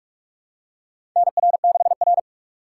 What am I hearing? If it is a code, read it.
Morse code, heard as a keyed tone, about 46 wpm, 690 Hz. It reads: NR6R
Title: USA Call Signs - 4 Characters - 8x Character Spacing - 45wpm